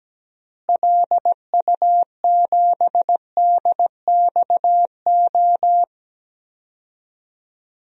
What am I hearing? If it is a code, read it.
Morse code, heard as a keyed tone, about 17 wpm, 700 Hz. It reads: LU7DXO
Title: DXCC-100 Call Signs - 6 Characters - Repeated 2x - 17wpm